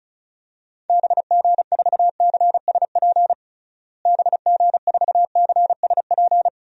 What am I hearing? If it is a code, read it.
Morse code, heard as a keyed tone, about 36 wpm, 700 Hz. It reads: BG4CSP BG4CSP
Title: DXCC-100 Call Signs - 6 Characters - Repeated 3x - 35wpm